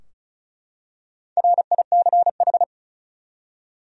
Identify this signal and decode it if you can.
Morse code, heard as a keyed tone, about 35 wpm, 700 Hz. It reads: RICH
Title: USA Common Names - Max 5 Letters - 35wpm